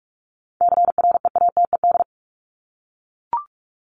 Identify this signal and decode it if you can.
Morse code, heard as a keyed tone, about 44 wpm, 700 Hz. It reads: CREATED
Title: ED Suffix - Top 100 Words with Base Words 45wpm